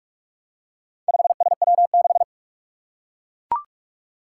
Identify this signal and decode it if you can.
Morse code, heard as a keyed tone, about 41 wpm, 700 Hz. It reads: 5SW6
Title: Four Challenging Letter-Number Combinations 45wpm